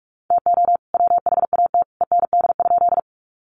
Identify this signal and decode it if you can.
Morse code, heard as a keyed tone, about 44 wpm, 700 Hz. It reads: TO WHAT END?
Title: Sentences from Top 200 Words 45wpm